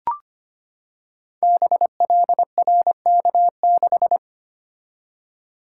Morse code, heard as a keyed tone, about 25 wpm, 700 Hz. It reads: BLRK6
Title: Code Groups - 5 Letter:Numbers - 25wpm